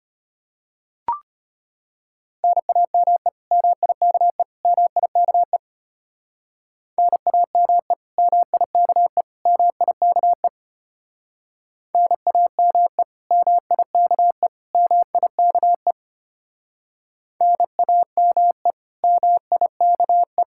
Morse code, heard as a keyed tone, about 31 wpm, 700 Hz. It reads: NAME MIKE MIKE NAME MIKE MIKE NAME MIKE MIKE NAME MIKE
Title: Speed-Racing Common QSO Phrases - 25wpm